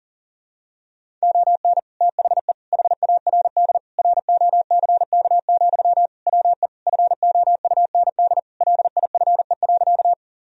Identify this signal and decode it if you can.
Morse code, heard as a keyed tone, about 40 wpm, 700 Hz. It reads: ON THE HARD ROCK, WE FOUND LIFE.
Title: Sentences from Top 1000 Words - Review Part 2 - 40wpm